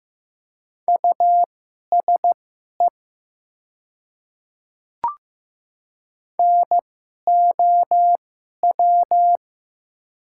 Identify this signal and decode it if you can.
Morse code, heard as a keyed tone, about 15 wpm, spaced as for 11 wpm, 700 Hz. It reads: USE NOW
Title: 2x Character Spacing - Top 100 Words - 15wpm